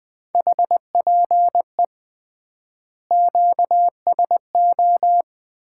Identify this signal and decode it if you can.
Morse code, heard as a keyed tone, about 21 wpm, 700 Hz. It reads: HPE QSO
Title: Common QSO Phrases - 3x Word-Spacing - 20wpm